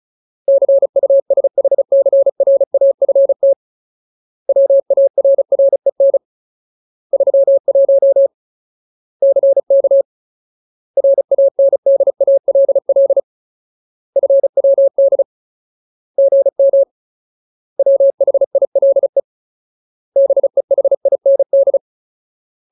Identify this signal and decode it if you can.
Morse code, heard as a keyed tone, about 34 wpm, 550 Hz. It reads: CUSHCRAFT WARREN 31 CK RANDALL FWD GM WHILE BEHIND
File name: Ultimate Word List / 35WPM Ultimate